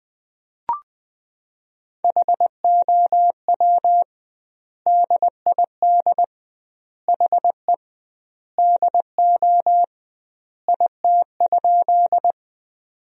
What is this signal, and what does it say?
Morse code, heard as a keyed tone, about 20 wpm, 700 Hz. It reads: HOW DID HE DO IT?
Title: 2x Word-Spacing - Sentences from Top 100 Words - 20wpm